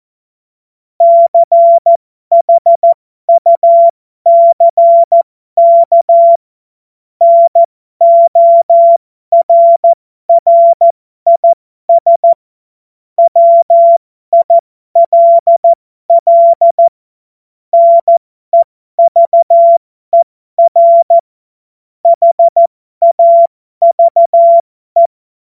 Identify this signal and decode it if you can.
Morse code, heard as a keyed tone, about 14 wpm, spaced as for 12 wpm, 680 Hz. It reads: CHUCK NORRIS WILL NEVER HAVE
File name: Morsekurs / Witze